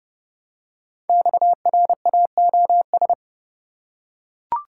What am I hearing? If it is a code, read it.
Morse code, heard as a keyed tone, about 30 wpm, 700 Hz. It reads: XRAOS
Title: Code Groups - 5 Letters - 30wpm